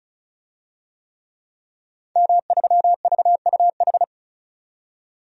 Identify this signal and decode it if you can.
Morse code, heard as a keyed tone, about 36 wpm, 700 Hz. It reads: M3VUH
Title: DXCC-100 Call Signs - 5 Characters - 35wpm